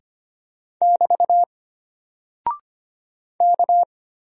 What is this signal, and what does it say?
Morse code, heard as a keyed tone, about 25 wpm, 700 Hz. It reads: <BT> K
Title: Single Character (Letter:Number:Punctuation:Prosign) Alt 25wpm